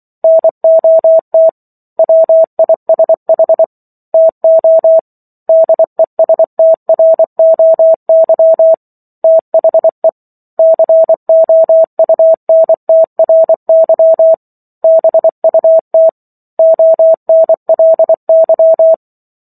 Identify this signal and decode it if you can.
Morse code, heard as a keyed tone, about 24 wpm, 650 Hz. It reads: NOT WISH TO DESTROY THE COUNTRY BUT ONLY